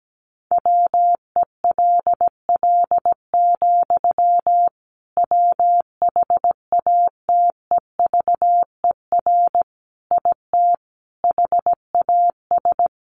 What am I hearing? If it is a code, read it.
Morse code, heard as a keyed tone, about 17 wpm, 700 Hz. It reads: WELL, WHATEVER IT HAS
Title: Sentences from Top 1100 Words 17wpm